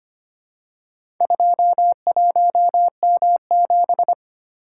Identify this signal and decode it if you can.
Morse code, heard as a keyed tone, about 25 wpm, 700 Hz. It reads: 21M7
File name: practice / 25wpm / Four Letter-Number Combinations - 25wpm